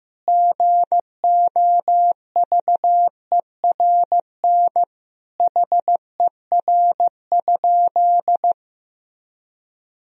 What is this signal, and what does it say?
Morse code, heard as a keyed tone, about 15 wpm, 700 Hz. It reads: GOVERN HER?
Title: Sentences from Top 1000 Words - Review Part 2 - 15wpm